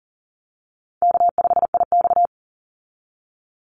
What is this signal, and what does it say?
Morse code, heard as a keyed tone, about 38 wpm, 700 Hz. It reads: K5IX
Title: Call Signs Repeated 2x - 40wpm